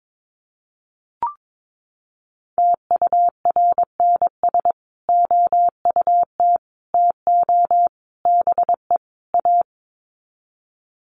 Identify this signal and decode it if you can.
Morse code, heard as a keyed tone, about 22 wpm, 700 Hz. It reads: TURNS OUT TO BE A
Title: Sets of 5 Words Encore 22wpm